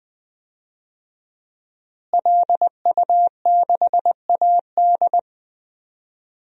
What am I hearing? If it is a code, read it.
Morse code, heard as a keyed tone, about 20 wpm, 700 Hz. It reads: LU6AD
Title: DXCC-100 Call Signs - 5 Characters - Repeated 3x - 20wpm